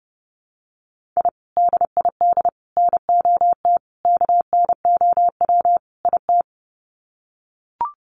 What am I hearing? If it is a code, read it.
Morse code, heard as a keyed tone, about 30 wpm, 700 Hz. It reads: I DID NOT KNOW IT
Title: Sets of 5 Words Encore2 30wpm